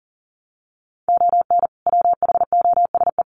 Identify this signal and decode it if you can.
Morse code, heard as a keyed tone, about 40 wpm, 700 Hz. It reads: ON WHOSE
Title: Sentences from Top 1200 Words 40wpm